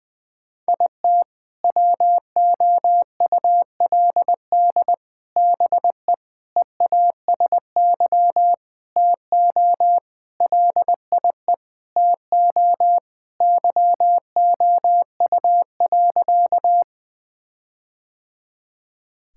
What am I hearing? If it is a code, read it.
Morse code, heard as a keyed tone, about 21 wpm, 700 Hz. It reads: IT WOULD BE EASY TO LIE TO YOU.
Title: Sentences from Top 1600 Words 20wpm